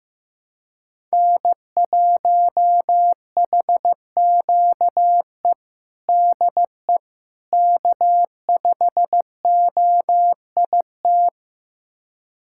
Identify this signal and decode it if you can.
Morse code, heard as a keyed tone, about 16 wpm, 700 Hz. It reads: N1HQE DE K5OIT
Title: USA Callsign DE Callsign - 5 Characters - 15wpm